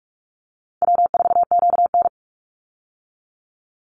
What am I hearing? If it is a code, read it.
Morse code, heard as a keyed tone, about 46 wpm, 700 Hz. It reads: W4QN
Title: Easy Call Signs - 45wpm